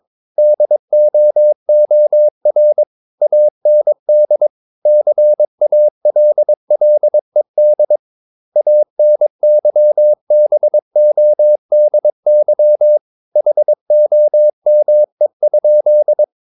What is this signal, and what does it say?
Morse code, heard as a keyed tone, about 22 wpm, 600 Hz. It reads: DOOR AND CALLED ANYBODY HOME?
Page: Romanen Never af Ken Follet på engelsk.